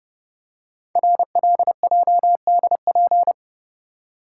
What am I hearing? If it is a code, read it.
Morse code, heard as a keyed tone, about 30 wpm, 700 Hz. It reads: RLJDP
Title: Code Groups - 5 Letters - 30wpm